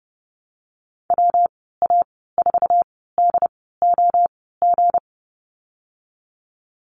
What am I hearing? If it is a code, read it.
Morse code, heard as a keyed tone, about 30 wpm, spaced as for 17 wpm, 700 Hz. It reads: WA4DOG
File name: Lesson 37g - Letter x - Call Signs 30wpm